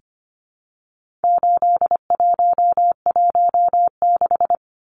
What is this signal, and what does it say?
Morse code, heard as a keyed tone, about 25 wpm, 700 Hz. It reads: 8116
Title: Numbers - 4-Digit 25wpm